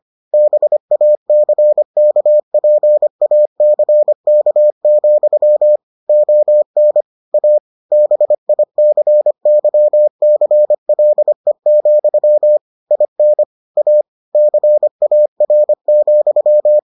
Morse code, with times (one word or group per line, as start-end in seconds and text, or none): 0.3-5.8 BACKPACK,
6.1-7.0 ON
7.3-7.6 A
7.9-12.6 BICYCLE,
12.9-13.4 IN
13.8-14.0 A
14.3-16.8 CAR,